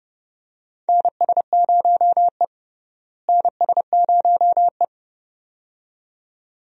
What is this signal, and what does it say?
Morse code, heard as a keyed tone, about 32 wpm, 700 Hz. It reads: NS0E NS0E
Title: USA Call Signs - 4 Characters - Repeated 2x - 30wpm